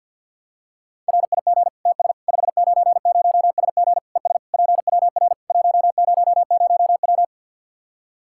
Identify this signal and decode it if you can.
Morse code, heard as a keyed tone, about 49 wpm, 700 Hz. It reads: RIG TS 590SG ES PWR 100W
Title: K1USN SST QSO Practice 50wpm